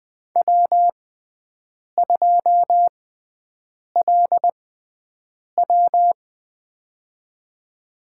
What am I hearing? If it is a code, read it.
Morse code, heard as a keyed tone, about 20 wpm, spaced as for 7 wpm, 700 Hz. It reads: W2LW
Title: USA Call Signs - 4 Characters - 6x Character Spacing - 20wpm